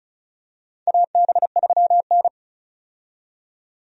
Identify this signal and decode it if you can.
Morse code, heard as a keyed tone, about 36 wpm, 700 Hz. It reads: AB3N
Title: USA Call Signs - 4 Characters - 35wpm